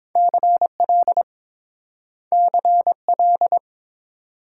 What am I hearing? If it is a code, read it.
Morse code, heard as a keyed tone, about 30 wpm, 700 Hz. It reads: CL CL